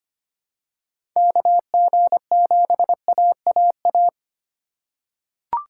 Morse code, heard as a keyed tone, about 25 wpm, 700 Hz. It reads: KG7AAA
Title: USA Call Signs - 3x Character Spacing - 25wpm